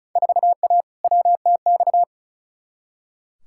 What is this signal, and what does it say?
Morse code, heard as a keyed tone, about 34 wpm, 700 Hz. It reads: 4A WTX
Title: Field Day Exchanges - 35wpm